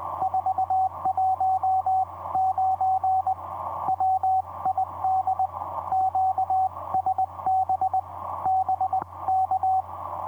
Morse code, heard as a keyed tone, about 20 wpm, 760 Hz, with radio noise. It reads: 419 WID QSB BK
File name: vp8cmh-mm on5jy etc 30m cw 23dec2010 2206